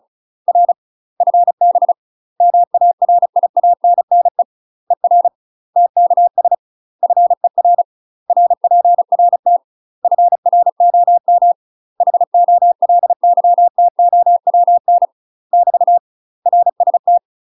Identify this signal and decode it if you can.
Morse code, heard as a keyed tone, about 34 wpm, 700 Hz. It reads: R FB MARIANNE ER TKS FER RPRT FROM HOLYTOWN <BT> RST
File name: UK MASTER QSOs TRAINING / 35WPM / UK MASTER QSOs 0011